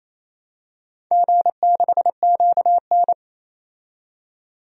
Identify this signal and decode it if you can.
Morse code, heard as a keyed tone, about 28 wpm, 700 Hz. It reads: G6QN
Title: DXCC-100 Call Signs - 4 Characters - 28wpm